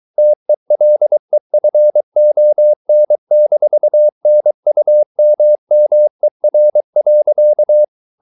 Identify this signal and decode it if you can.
Morse code, heard as a keyed tone, about 23 wpm, 600 Hz. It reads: TELEFON-NUMMER.
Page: Ferie 23 wpm